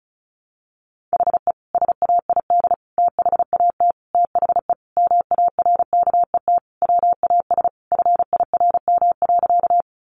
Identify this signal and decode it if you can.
Morse code, heard as a keyed tone, about 34 wpm, 700 Hz. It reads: HE SAID THAT THE MARKET WAS FIRM.
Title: Sentences from Top 1400 Words 35wpm